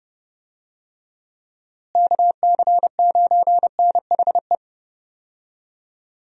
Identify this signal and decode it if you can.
Morse code, heard as a keyed tone, about 31 wpm, 700 Hz. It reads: KC9NHE
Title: USA Call Signs - 6 Characters - Repeated 3x - 30wpm